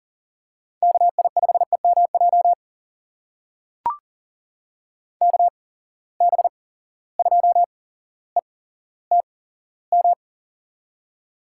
Morse code, heard as a keyed tone, about 38 wpm, spaced as for 17 wpm, 700 Hz. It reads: KI5EMJ KB2ETM
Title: USA Call Signs - 6 Characters - 8x Character Spacing - 40wpm